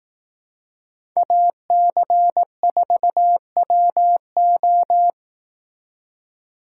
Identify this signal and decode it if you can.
Morse code, heard as a keyed tone, about 18 wpm, 700 Hz. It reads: AC4WO
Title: Speed Racing - USA 2x2 Call Signs 15wpm